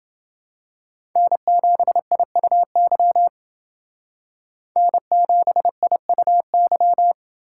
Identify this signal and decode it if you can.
Morse code, heard as a keyed tone, about 31 wpm, 700 Hz. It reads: N7IUY N7IUY